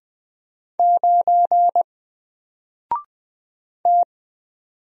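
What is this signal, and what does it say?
Morse code, heard as a keyed tone, about 20 wpm, 700 Hz. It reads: 9 T
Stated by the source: Single letter and number Morse drill at 20 wpm, from N4LCD.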